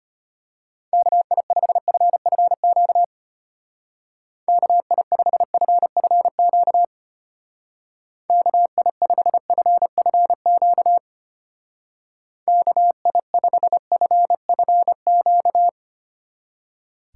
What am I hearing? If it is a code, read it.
Morse code, heard as a keyed tone, about 33 wpm, 700 Hz. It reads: KI5FFQ KI5FFQ KI5FFQ KI5FFQ